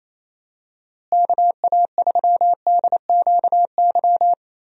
Morse code, heard as a keyed tone, about 28 wpm, 700 Hz. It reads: KA3DQY
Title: USA Call Signs - 6 Characters - Repeated 3x - 28wpm